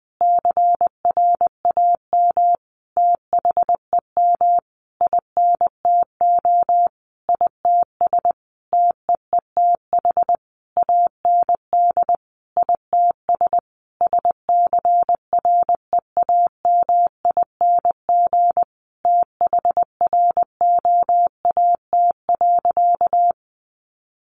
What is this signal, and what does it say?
Morse code, heard as a keyed tone, about 20 wpm, 700 Hz. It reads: CRAM THEM INTO ITS TEETH AND ITS SCREAMING THROAT.
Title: A Sound Of Thunder 20wpm